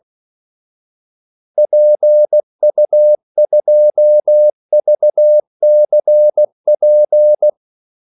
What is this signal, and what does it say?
Morse code, heard as a keyed tone, about 16 wpm, 600 Hz. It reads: PU2VCP
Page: Kaldesignaler 16 wpm